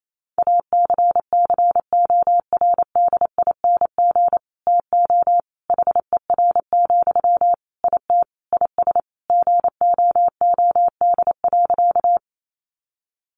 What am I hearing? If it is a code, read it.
Morse code, heard as a keyed tone, about 28 wpm, 700 Hz. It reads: ACCORDING TO HER, IT IS GOOD.
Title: Sentences from Top 800 Words 28wpm